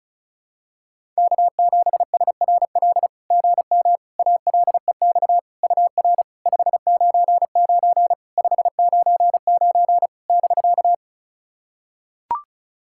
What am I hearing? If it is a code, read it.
Morse code, heard as a keyed tone, about 36 wpm, 660 Hz. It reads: K7SRL GM ALEX UR 599 599 <BK>
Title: SOTA QSO Practice 35wpm